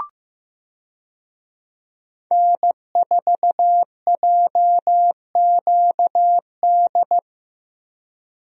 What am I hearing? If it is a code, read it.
Morse code, heard as a keyed tone, about 16 wpm, 700 Hz. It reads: N4JQD
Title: USA Call Signs - 5 Characters - Repeated 3x - 15wpm